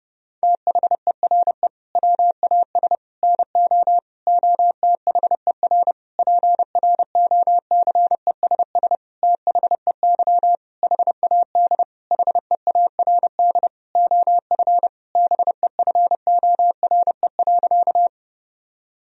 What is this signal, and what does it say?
Morse code, heard as a keyed tone, about 30 wpm, 700 Hz. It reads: THERE WAS NO OTHER PROCESS THEY HAD HEARD OF BEFORE.
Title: Sentences from Top 1000 Words - Review Part 3 - 30wpm